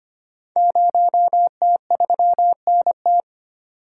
Morse code, heard as a keyed tone, about 25 wpm, 700 Hz. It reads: 0T3NT
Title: Code Groups - 5 Letter:Numbers - 25wpm